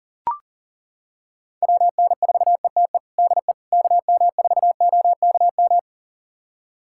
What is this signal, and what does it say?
Morse code, heard as a keyed tone, about 37 wpm, 700 Hz. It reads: WN4ETE DE KM4OKM
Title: USA Callsign DE Callsign 40wpm